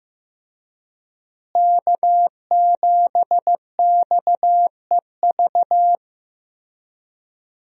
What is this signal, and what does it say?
Morse code, heard as a keyed tone, about 15 wpm, 700 Hz. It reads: K7XEV
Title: USA Call Signs - 5 Characters - Repeated 2x - 15wpm